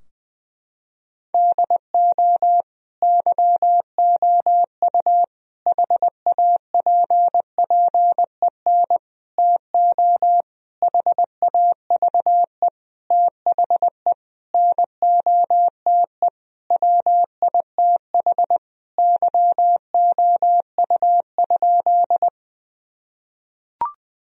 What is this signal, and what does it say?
Morse code, heard as a keyed tone, about 20 wpm, 700 Hz. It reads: DO YOU HAPPEN TO HAVE THE NOTE WITH YOU?
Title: Sentences from Top 1000 Words - Review Part 2 - 20wpm